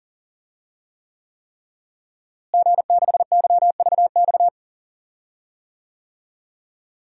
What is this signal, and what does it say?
Morse code, heard as a keyed tone, about 40 wpm, 700 Hz. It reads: G6YVX
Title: DXCC-100 Call Signs - 5 Characters - Repeated 3x - 40wpm